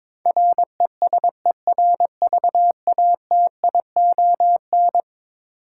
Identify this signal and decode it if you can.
Morse code, heard as a keyed tone, about 22 wpm, 700 Hz. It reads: RESERVATION
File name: practice / 22wpm / Eleven Letter Words - 22wpm